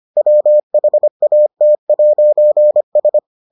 Morse code, heard as a keyed tone, about 25 wpm, 600 Hz. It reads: WHAT'S